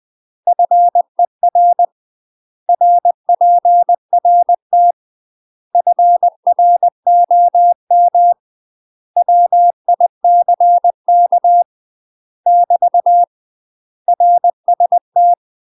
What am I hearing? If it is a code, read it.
Morse code, heard as a keyed tone, about 20 wpm, 700 Hz. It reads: FER RPRT FROM WICK <BT> RST
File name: UK MASTER QSOs TRAINING / 20WPM / UK MASTER QSOs 0044